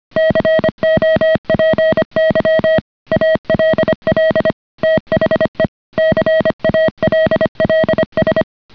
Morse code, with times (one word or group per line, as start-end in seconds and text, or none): 0.2-2.8 COPY
3.1-4.5 ALL
4.8-5.7 THE
6.0-8.4 CALLS